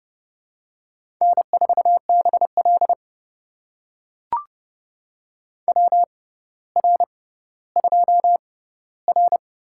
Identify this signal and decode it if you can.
Morse code, heard as a keyed tone, about 30 wpm, spaced as for 15 wpm, 700 Hz. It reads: N4BL WR2R
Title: USA Call Signs - 4 Characters - 6x Character Spacing - 30wpm